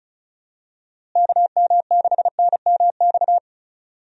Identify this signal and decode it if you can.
Morse code, heard as a keyed tone, about 36 wpm, 700 Hz. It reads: KM6NMX
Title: USA Call Signs - 6 Characters - Repeated 2x - 35wpm